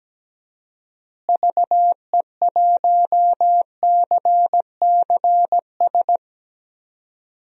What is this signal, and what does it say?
Morse code, heard as a keyed tone, about 17 wpm, 700 Hz. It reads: VE1CCS
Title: DXCC-100 Call Signs - 6 Characters - Repeated 2x - 17wpm